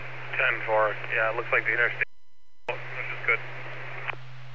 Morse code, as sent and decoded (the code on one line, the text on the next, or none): none